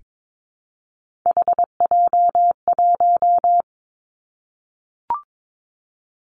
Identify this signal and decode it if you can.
Morse code, heard as a keyed tone, about 22 wpm, 700 Hz. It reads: HJ1
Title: Three Challenging Letter-Number Combinations 22wpm